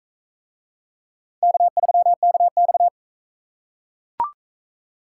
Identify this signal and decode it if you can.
Morse code, heard as a keyed tone, about 41 wpm, 700 Hz. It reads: K3KX